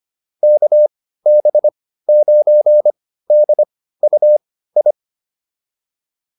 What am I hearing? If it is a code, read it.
Morse code, heard as a keyed tone, about 25 wpm, spaced as for 15 wpm, 600 Hz. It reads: KB9DUI
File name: QSO308 15